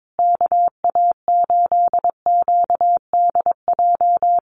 Morse code, heard as a keyed tone, about 22 wpm, 700 Hz. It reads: KA8QDJ
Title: USA Call Signs Repeated 3x 22wpm